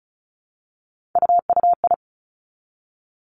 Morse code, heard as a keyed tone, about 36 wpm, 700 Hz. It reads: UUI
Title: Three Challenging Letter-Number Combinations 35wpm